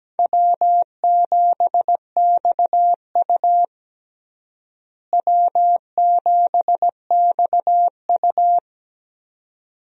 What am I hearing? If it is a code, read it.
Morse code, heard as a keyed tone, about 17 wpm, 700 Hz. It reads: W7XU W7XU
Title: USA Call Signs - 4 Characters - Repeated 2x - 17wpm